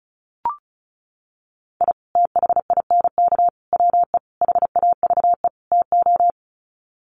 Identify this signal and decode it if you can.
Morse code, heard as a keyed tone, about 34 wpm, 700 Hz. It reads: I THINK WE HAVE TO
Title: Sets of 5 Words 35wpm